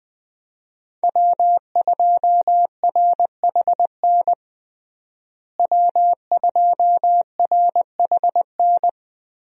Morse code, heard as a keyed tone, about 21 wpm, 700 Hz. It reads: W2RHN W2RHN
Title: USA Call Signs - 5 Characters - Repeated 2x - 20wpm